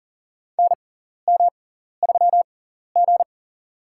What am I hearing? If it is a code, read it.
Morse code, heard as a keyed tone, about 40 wpm, spaced as for 14 wpm, 700 Hz. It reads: NM3G
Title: USA Call Signs - 4 Characters - 6x Character Spacing - 40wpm